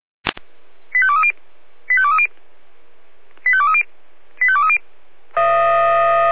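Five-tone selective call (ZVEI), heard as quick runs of five tones, tones of about 70 ms. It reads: 85230 85230 85230 85230
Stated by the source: Five-tone ZVEI selective call received on an SDR.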